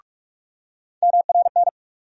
Morse code, heard as a keyed tone, about 41 wpm, 700 Hz. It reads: MRN
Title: USA Call Sign Postfixes - 3 Letters 45wpm